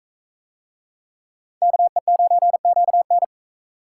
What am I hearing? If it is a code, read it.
Morse code, heard as a keyed tone, about 36 wpm, 700 Hz. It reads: KE9QN